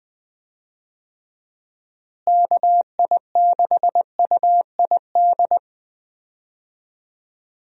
Morse code, heard as a keyed tone, about 20 wpm, 700 Hz. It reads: KI6UID
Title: USA Call Signs - 6 Characters - Repeated 2x - 20wpm